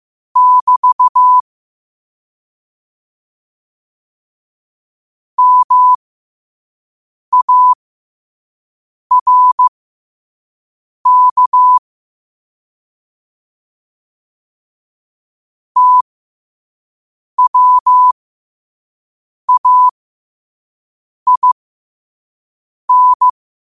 Morse code, heard as a keyed tone, about 15 wpm, spaced as for 5 wpm, 1000 Hz. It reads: <BT> MARK TWAIN
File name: qotdfiles / 5wpm